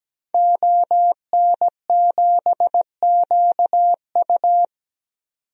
Morse code, heard as a keyed tone, about 17 wpm, 700 Hz. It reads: ON7QU
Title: DXCC-100 Call Signs - 5 Characters - 17wpm